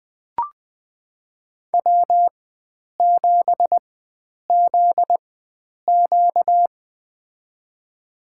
Morse code, heard as a keyed tone, about 20 wpm, spaced as for 9 wpm, 700 Hz. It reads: W7ZQ
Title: USA Call Signs - 4 Characters - 4x Character Spacing - 20wpm